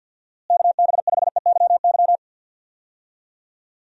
Morse code, heard as a keyed tone, about 51 wpm, 700 Hz. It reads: KB5EYY
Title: USA Call Signs - 6 Characters - Repeated 2x - 50wpm